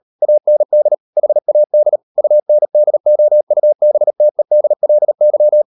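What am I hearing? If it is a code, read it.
Morse code, heard as a keyed tone, about 38 wpm, 600 Hz. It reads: AND HAD UNDOUBTEDLY
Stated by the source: Morse code practice at 38 WPM, sending Ken Follett's novel Never in English, from CW med Gnister.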